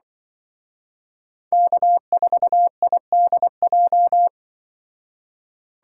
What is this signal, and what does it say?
Morse code, heard as a keyed tone, about 25 wpm, 700 Hz. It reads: K4IDJ
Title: Speed Racing - USA 1x3 Call Signs 20wpm